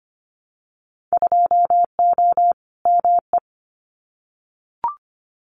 Morse code, heard as a keyed tone, about 25 wpm, 700 Hz. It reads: 2O ME
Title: Winter Field Day Exchanges 25wpm